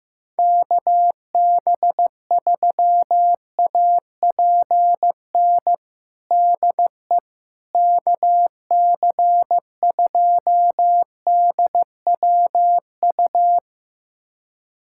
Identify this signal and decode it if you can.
Morse code, heard as a keyed tone, about 15 wpm, 700 Hz. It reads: KB3APN DE KC2DWU
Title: USA Callsign DE Callsign - 6 Characters - 15wpm